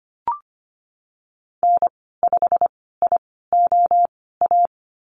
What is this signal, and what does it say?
Morse code, heard as a keyed tone, about 25 wpm, spaced as for 16 wpm, 700 Hz. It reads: N5IOA
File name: Lesson 15g - Number 5 - Call Signs 25wpm